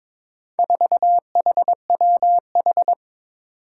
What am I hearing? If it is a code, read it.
Morse code, heard as a keyed tone, about 22 wpm, 700 Hz. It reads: 4HWH
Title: Four Challenging Letter-Number Combinations 22wpm